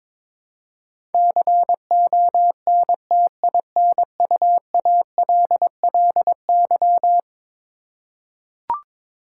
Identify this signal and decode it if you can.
Morse code, heard as a keyed tone, about 23 wpm, 700 Hz. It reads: CONTINUALLY
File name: practice / 22wpm / Eleven Letter Words - 22wpm